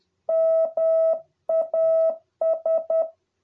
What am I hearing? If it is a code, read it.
Morse code, heard as a keyed tone, about 10 wpm, 650 Hz. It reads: MAS